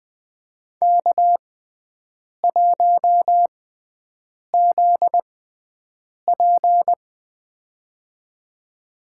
Morse code, heard as a keyed tone, about 20 wpm, spaced as for 7 wpm, 700 Hz. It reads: K1ZP
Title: USA Call Signs - 4 Characters - 6x Character Spacing - 20wpm